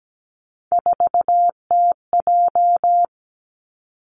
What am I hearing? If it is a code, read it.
Morse code, heard as a keyed tone, about 17 wpm, 700 Hz. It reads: 4TJ
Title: Three Challenging Letter-Number Combinations 17wpm